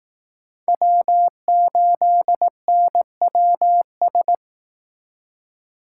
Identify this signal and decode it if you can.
Morse code, heard as a keyed tone, about 18 wpm, 700 Hz. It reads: W8NWS
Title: Speed Racing - USA 1x3 Call Signs 15wpm